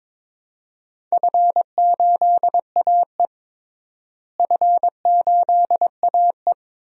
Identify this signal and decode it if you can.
Morse code, heard as a keyed tone, about 22 wpm, 700 Hz. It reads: F8AE F8AE
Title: DXCC-100 Call Signs - 4 Characters - Repeated 2x - 22wpm